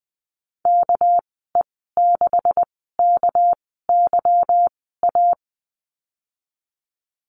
Morse code, heard as a keyed tone, about 20 wpm, spaced as for 15 wpm, 700 Hz. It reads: KE6KYA
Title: 200 Lesson 32g - Letter k - Call Signs 20wpm